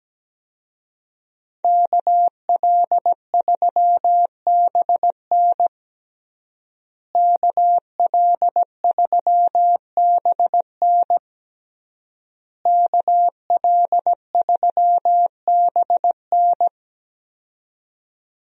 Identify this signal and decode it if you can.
Morse code, heard as a keyed tone, about 17 wpm, 700 Hz. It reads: KL3BN KL3BN KL3BN
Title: USA Call Signs - 5 Characters - Repeated 3x - 17wpm